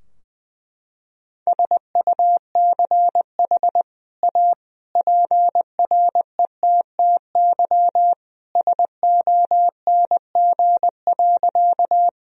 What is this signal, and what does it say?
Morse code, heard as a keyed tone, about 20 wpm, 700 Hz. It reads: SUCH A PRETTY SONG.
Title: Sentences from Top 1000 Words - Review Part 2 - 20wpm